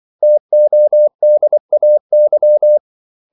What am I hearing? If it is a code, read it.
Morse code, heard as a keyed tone, about 24 wpm, 600 Hz. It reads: TODAY
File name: CW / Practice / 500MostCommonEnglishWords24